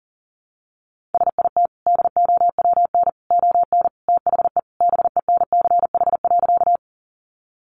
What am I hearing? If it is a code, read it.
Morse code, heard as a keyed tone, about 40 wpm, 700 Hz. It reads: SIT DOWN ON THE BENCH.
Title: Sentences from Top 1400 Words 40wpm